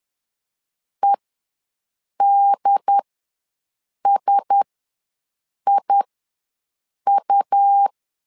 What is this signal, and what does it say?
Morse code, heard as a keyed tone, about 10 wpm, spaced as for 6 wpm, 800 Hz. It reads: EDSIU
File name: LEZIONI / 4 Q1S ws 10wpm 2208